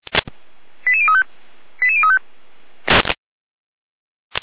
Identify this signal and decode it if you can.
Five-tone selective call (ZVEI), heard as quick runs of five tones, tones of about 70 ms. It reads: 88035 88035